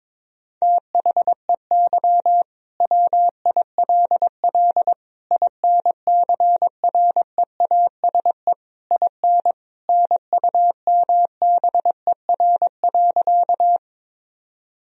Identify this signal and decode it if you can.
Morse code, heard as a keyed tone, about 21 wpm, 700 Hz. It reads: THEY WILL INCREASE IN NUMBER.
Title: Sentences from Top 1000 Words - Review Part 4 - 22wpm